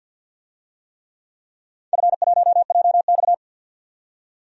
Morse code, heard as a keyed tone, about 53 wpm, 700 Hz. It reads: F1JX
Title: DXCC-100 Call Signs - 4 Characters - 50wpm